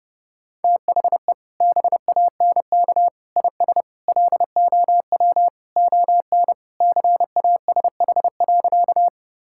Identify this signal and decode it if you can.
Morse code, heard as a keyed tone, about 30 wpm, 700 Hz. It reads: THE BANK IS LOW ON CASH.
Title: Sentences from Top 1000 Words - Review Part 5 - 30wpm